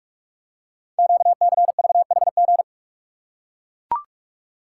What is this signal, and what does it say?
Morse code, heard as a keyed tone, about 44 wpm, 700 Hz. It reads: QCVHG
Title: Code Groups - 5 Letters - 45wpm